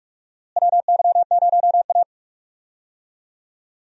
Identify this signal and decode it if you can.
Morse code, heard as a keyed tone, about 46 wpm, 700 Hz. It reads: WY0A
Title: USA Call Signs - 4 Characters - Repeated 2x - 45wpm